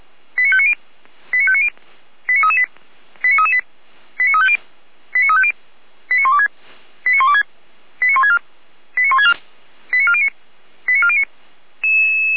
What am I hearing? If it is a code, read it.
Five-tone selective call (ZVEI), heard as quick runs of five tones, tones of about 70 ms. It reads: 89590 89590 89338 89338 89350 89350 89126 89126 89164 89164 89409 89409
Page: /455/SDR/ZVEI/89701/2023.01.11/